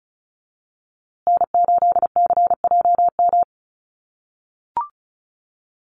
Morse code, heard as a keyed tone, about 34 wpm, 700 Hz. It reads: N8CJM
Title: USA Call Signs Repeated 3x 35wpm